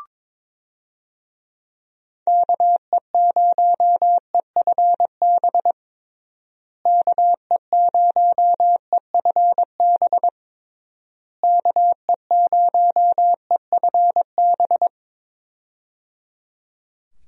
Morse code, heard as a keyed tone, about 22 wpm, 700 Hz. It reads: KE0EFB KE0EFB KE0EFB
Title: USA Call Signs - 6 Characters - Repeated 3x - 22wpm